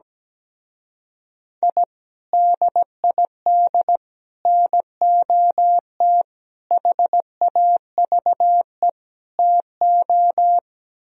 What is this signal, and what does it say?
Morse code, heard as a keyed tone, about 17 wpm, 700 Hz. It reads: I DID NOT HAVE TO
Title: Speed-Racing - Sets of 5 Words 17wpm